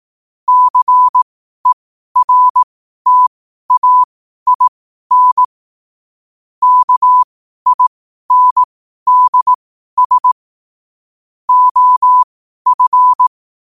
Morse code, heard as a keyed tone, about 18 wpm, spaced as for 12 wpm, 1000 Hz. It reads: CERTAIN KINDS OF
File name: qotdfiles / 12wpm